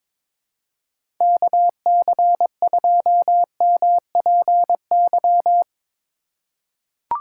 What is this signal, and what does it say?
Morse code, heard as a keyed tone, about 22 wpm, 700 Hz. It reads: KC2MPY
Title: USA Call Signs - 6 Characters - 6x Character Spacing - 22wpm